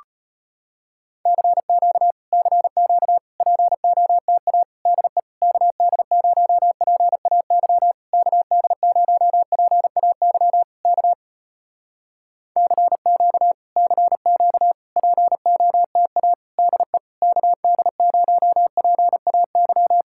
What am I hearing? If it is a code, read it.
Morse code, heard as a keyed tone, about 34 wpm, 700 Hz. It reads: CQ CQ POTA DE KD0PAY KD0PAY K CQ CQ POTA DE KD0PAY